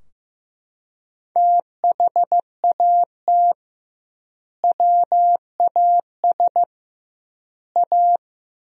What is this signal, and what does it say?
Morse code, heard as a keyed tone, about 15 wpm, 700 Hz. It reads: THAT WAS A
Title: 2x Word-Spacing - Sentences from Top 200 Words - 15wpm